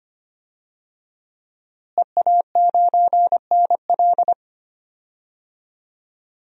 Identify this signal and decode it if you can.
Morse code, heard as a keyed tone, about 25 wpm, 700 Hz. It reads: EA9NL
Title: DXCC-100 Call Signs - 5 Characters - Repeated 3x - 25wpm